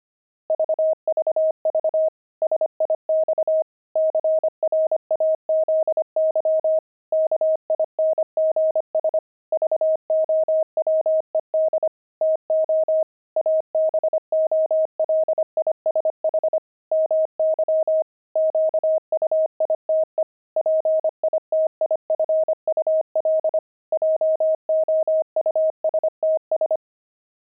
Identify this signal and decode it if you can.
Morse code, heard as a keyed tone, about 25 wpm, 630 Hz. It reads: VVV SIX CRAZY KINGS VOWED TO ABOLISH MY QUITE PITIFUL JOUSTS
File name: pangram-fs-25-cs-25-crazy